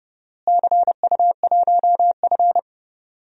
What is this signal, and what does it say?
Morse code, heard as a keyed tone, about 32 wpm, 700 Hz. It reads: CU1F
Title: DXCC-100 Call Signs - 4 Characters - Repeated 2x - 30wpm